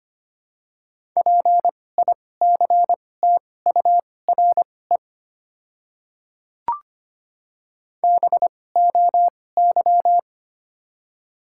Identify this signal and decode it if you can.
Morse code, heard as a keyed tone, about 25 wpm, spaced as for 18 wpm, 700 Hz. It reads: PICTURE BOY
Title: 2x Character Spacing - Top 300 Words - 25wpm